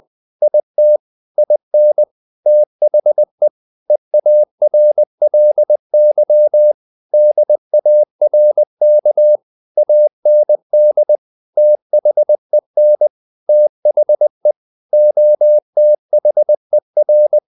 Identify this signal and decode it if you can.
Morse code, heard as a keyed tone, about 20 wpm, 600 Hz. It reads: IT IN THE EARLY DARK AND THEN THE OTHER
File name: the old man and the sea part 2 20wpm